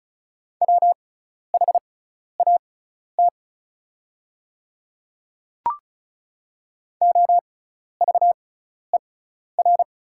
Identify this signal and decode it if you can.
Morse code, heard as a keyed tone, about 36 wpm, spaced as for 12 wpm, 700 Hz. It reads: WHAT OVER
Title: 6x Character Spacing - Top 100 Words - 35wpm